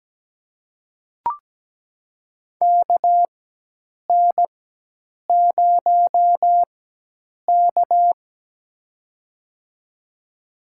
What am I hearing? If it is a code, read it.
Morse code, heard as a keyed tone, about 17 wpm, spaced as for 8 wpm, 700 Hz. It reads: KN0K
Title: USA Call Signs - 4x Character Spacing - 17wpm